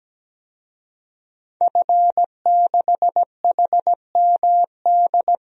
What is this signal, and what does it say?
Morse code, heard as a keyed tone, about 17 wpm, 700 Hz. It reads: F6HMD
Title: DXCC-100 Call Signs - 5 Characters - 17wpm